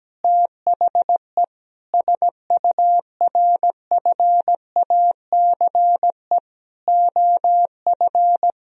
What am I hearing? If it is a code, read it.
Morse code, heard as a keyed tone, about 17 wpm, 700 Hz. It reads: THE SURFACE OF
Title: Sentences from Top 1000 Words - Review Part 3 - 17wpm